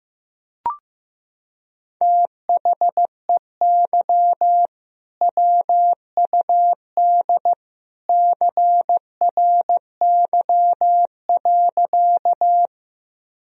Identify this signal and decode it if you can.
Morse code, heard as a keyed tone, about 15 wpm, 700 Hz. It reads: THEY WUD CRY.
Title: Sentences from Top 500 Words and Abbreviations 15wpm